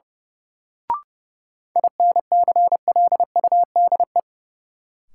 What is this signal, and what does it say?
Morse code, heard as a keyed tone, about 31 wpm, 700 Hz. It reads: INCLUDE
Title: Sentences from Top 1000 Words - Review Part 3 - New Word Drill - 30wpm